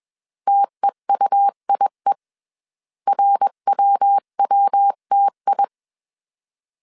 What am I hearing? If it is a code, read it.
Morse code, heard as a keyed tone, about 21 wpm, 800 Hz. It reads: TEUIE RWWTI